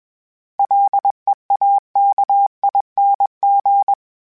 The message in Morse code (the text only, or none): LEAKING